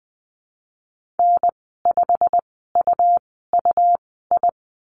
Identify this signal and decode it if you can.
Morse code, heard as a keyed tone, about 20 wpm, spaced as for 15 wpm, 700 Hz. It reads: N5UUI
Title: 122 Lesson 20g - Number 3 - Call Signs 20wpm